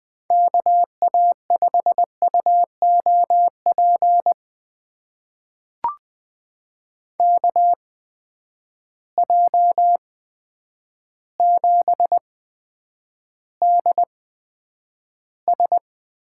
Morse code, heard as a keyed tone, about 20 wpm, spaced as for 9 wpm, 700 Hz. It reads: KA5UOP KJ7DS
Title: USA Call Signs - 6 Characters - 8x Character Spacing - 20wpm